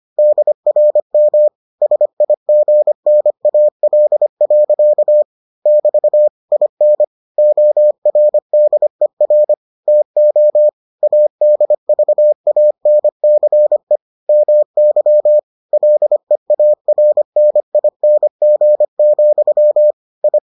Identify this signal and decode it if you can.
Morse code, heard as a keyed tone, about 25 wpm, 600 Hz. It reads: DRM SIGNAL. <BT> IN ORDER TO ADVANCE MY LEARNING, I